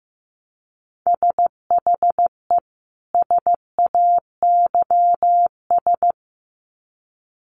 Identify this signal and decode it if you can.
Morse code, heard as a keyed tone, about 15 wpm, 700 Hz. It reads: SHE SAYS
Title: Sets of 2 Words 15wpm